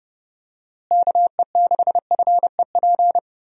Morse code, heard as a keyed tone, about 30 wpm, 700 Hz. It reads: KE6FEP
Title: USA Call Signs - 6 Characters - Repeated 2x - 30wpm